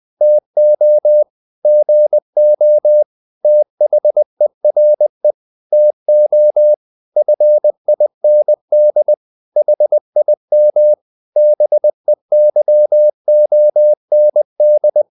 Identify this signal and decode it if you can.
Morse code, heard as a keyed tone, about 20 wpm, 600 Hz. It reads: TO GO THERE TO FIND HIM BEYOND
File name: the old man and the sea / the old man and the sea part 1 20wpm